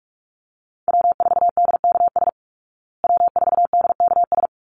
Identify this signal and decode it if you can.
Morse code, heard as a keyed tone, about 44 wpm, 700 Hz. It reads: W4DKS W4DKS
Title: Call Signs Repeated 3x - 45wpm